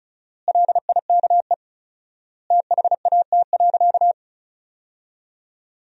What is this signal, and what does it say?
Morse code, heard as a keyed tone, about 34 wpm, 700 Hz. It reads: LIKE THAT.
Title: 4x Word-Spacing - Sentences from Top 300 Words - 35wpm